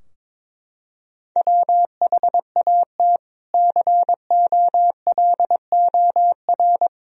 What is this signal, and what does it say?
Morse code, heard as a keyed tone, about 22 wpm, 700 Hz. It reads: WHAT COLOR
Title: Sentences from Top 1000 Words - Review Part 3 - 22wpm